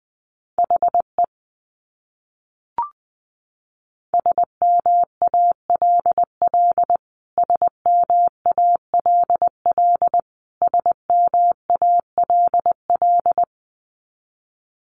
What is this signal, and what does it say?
Morse code, heard as a keyed tone, about 20 wpm, 700 Hz. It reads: HE SMALL SMALL SMALL
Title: Top 300 Words Repeated 20wpm